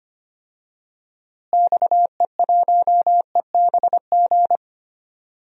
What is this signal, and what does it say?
Morse code, heard as a keyed tone, about 25 wpm, 700 Hz. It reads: XE1EBG
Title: DXCC-100 Call Signs - 6 Characters - Repeated 2x - 25wpm